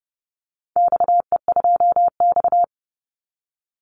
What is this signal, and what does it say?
Morse code, heard as a keyed tone, about 29 wpm, 700 Hz. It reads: XE2X
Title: Call Signs Repeated 3x - 30wpm